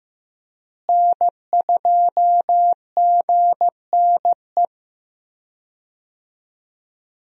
Morse code, heard as a keyed tone, about 15 wpm, 700 Hz. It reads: N2GNE
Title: USA Call Signs - 5 Characters - Repeated 3x - 15wpm